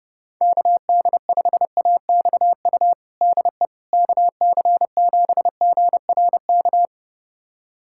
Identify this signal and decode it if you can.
Morse code, heard as a keyed tone, about 30 wpm, 700 Hz. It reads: KD5AXU DE KC7GRK
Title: USA Callsign DE Callsign - 6 Characters - 30wpm